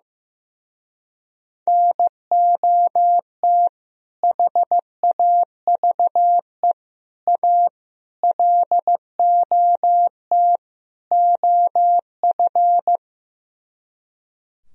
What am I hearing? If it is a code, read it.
Morse code, heard as a keyed tone, about 16 wpm, 700 Hz. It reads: NOT HAVE A LOT OF
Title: Speed-Racing - Sets of 5 Words 15wpm